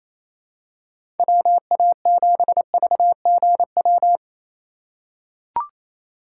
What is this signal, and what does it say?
Morse code, heard as a keyed tone, about 28 wpm, 700 Hz. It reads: WA7VGW
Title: USA Call Signs - 6 Characters - 2x Character Spacing - 28wpm